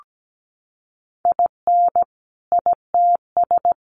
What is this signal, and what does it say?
Morse code, heard as a keyed tone, about 17 wpm, 700 Hz. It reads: IN ITS
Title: Sentences from Top 1400 Words 17wpm